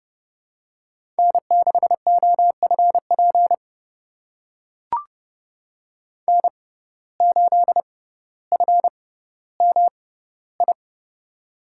Morse code, heard as a keyed tone, about 32 wpm, spaced as for 16 wpm, 700 Hz. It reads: N6OFP N8FMI
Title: USA Call Signs - 5 Characters - 6x Character Spacing - 30wpm